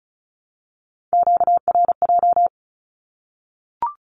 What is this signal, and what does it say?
Morse code, heard as a keyed tone, about 34 wpm, 700 Hz. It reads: QRJ
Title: Q-Codes Repeated 3x 35wpm